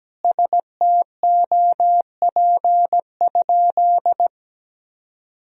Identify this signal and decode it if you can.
Morse code, heard as a keyed tone, about 17 wpm, 700 Hz. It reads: STOP?
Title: Sentences from Top 1000 Words - Review Part 1 - 17wpm